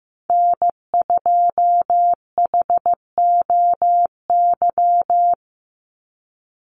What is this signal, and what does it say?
Morse code, heard as a keyed tone, about 15 wpm, 700 Hz. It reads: N2HOY
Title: USA Call Signs 15wpm